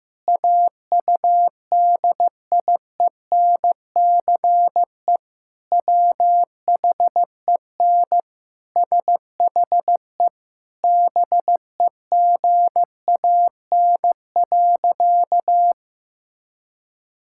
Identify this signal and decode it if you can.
Morse code, heard as a keyed tone, about 15 wpm, 700 Hz. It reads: AUDIENCE WHEN SHE BEGAN.
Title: Sentences from Top 1000 Words - Review Part 5 - 15wpm